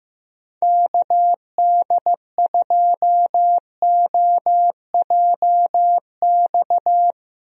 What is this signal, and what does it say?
Morse code, heard as a keyed tone, about 16 wpm, 700 Hz. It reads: KD2OJX
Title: USA Call Signs - 6 Characters - 15wpm